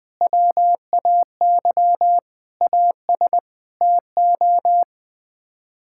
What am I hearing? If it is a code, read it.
Morse code, heard as a keyed tone, about 20 wpm, 700 Hz. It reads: WAY AS TO
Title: Sets of 6 Words - 20wpm